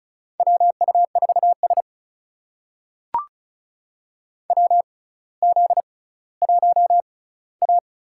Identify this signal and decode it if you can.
Morse code, heard as a keyed tone, about 36 wpm, spaced as for 18 wpm, 700 Hz. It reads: WU4S WZ1A
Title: USA Call Signs - 4 Characters - 6x Character Spacing - 35wpm